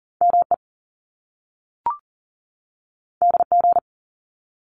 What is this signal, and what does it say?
Morse code, heard as a keyed tone, about 38 wpm, 700 Hz. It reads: ME DG
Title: Top Two Letter Combinations in Words 40wpm